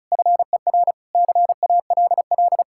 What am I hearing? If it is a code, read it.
Morse code, heard as a keyed tone, about 43 wpm, 700 Hz. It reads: FER CALL
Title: Speed-Racing Common QSO Phrases - 35wpm